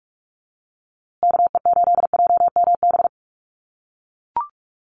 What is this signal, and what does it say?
Morse code, heard as a keyed tone, about 44 wpm, 700 Hz. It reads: KE8JMB
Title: USA Call Signs Repeated 3x 45wpm